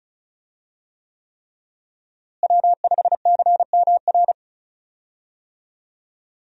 Morse code, heard as a keyed tone, about 36 wpm, 700 Hz. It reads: W5CMR
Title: USA Call Signs - 5 Characters - Repeated 3x - 35wpm